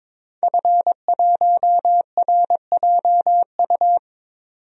Morse code, heard as a keyed tone, about 22 wpm, 700 Hz. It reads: F1RJU
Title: DXCC-100 Call Signs - 5 Characters - 22wpm